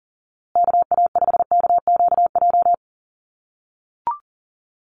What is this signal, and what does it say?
Morse code, heard as a keyed tone, about 40 wpm, 700 Hz. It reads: KA5KQJ
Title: USA Call Signs Repeated 2x 40wpm